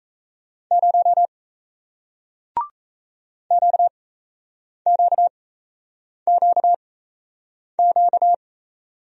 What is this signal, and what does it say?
Morse code, heard as a keyed tone, about 38 wpm, spaced as for 8 wpm, 700 Hz. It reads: 0 QQQQ